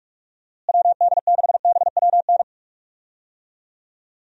Morse code, heard as a keyed tone, about 46 wpm, 700 Hz. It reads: WD6BWN
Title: USA Call Signs - 6 Characters - Repeated 2x - 45wpm